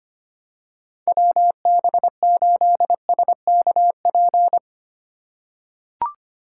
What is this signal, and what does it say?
Morse code, heard as a keyed tone, about 25 wpm, 700 Hz. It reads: WB8SKP
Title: K1USN SST QSO Practice - 25wpm